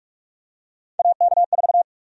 Morse code, heard as a keyed tone, about 46 wpm, 700 Hz. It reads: AK4
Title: USA Call Sign Prefixes - 45wpm